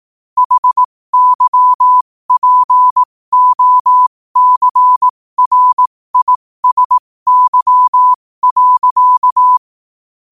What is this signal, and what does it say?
Morse code, heard as a keyed tone, about 18 wpm, spaced as for 15 wpm, 1000 Hz. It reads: HYPOCRISY.
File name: qotdfiles / 15wpm